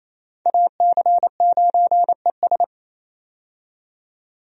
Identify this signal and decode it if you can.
Morse code, heard as a keyed tone, about 27 wpm, 700 Hz. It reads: AC9ES
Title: USA Call Signs - 5 Characters - 28wpm